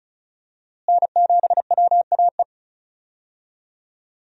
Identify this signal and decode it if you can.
Morse code, heard as a keyed tone, about 36 wpm, 700 Hz. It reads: N7WAE
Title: USA Call Signs - 5 Characters - Repeated 2x - 35wpm